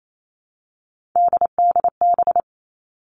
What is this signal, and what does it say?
Morse code, heard as a keyed tone, about 28 wpm, 700 Hz. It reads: DDB
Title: All Cut Numbers - 3-Digit 28wpm